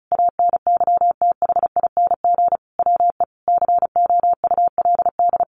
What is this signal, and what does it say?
Morse code, heard as a keyed tone, about 34 wpm, 700 Hz. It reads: ANYTHING WE COULD
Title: Sentences from Top 1400 Words 35wpm